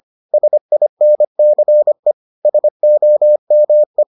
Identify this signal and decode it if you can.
Morse code, heard as a keyed tone, about 25 wpm, 600 Hz. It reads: SINCE SOME